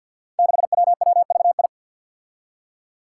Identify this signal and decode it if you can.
Morse code, heard as a keyed tone, about 52 wpm, 700 Hz. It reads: 6WWVI
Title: Five Challenging Letter-Number Combinations 50wpm